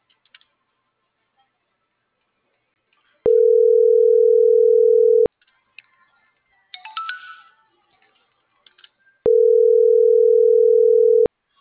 Ringback tone (North America), 440 + 480 Hz.